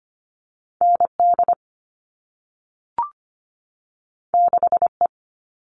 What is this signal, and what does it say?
Morse code, heard as a keyed tone, about 25 wpm, 700 Hz. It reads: ND 6E
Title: All Cut Numbers - 2-Digit 25wpm